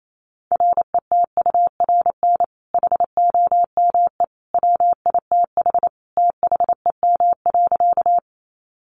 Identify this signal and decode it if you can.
Morse code, heard as a keyed tone, about 28 wpm, 700 Hz. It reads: RETURN HOME WITH THEM.
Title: Sentences from Top 800 Words 28wpm